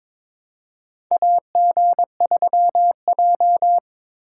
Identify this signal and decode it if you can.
Morse code, heard as a keyed tone, about 22 wpm, 700 Hz. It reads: AG3J
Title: USA Call Signs - 4 Characters - 2x Character Spacing - 22wpm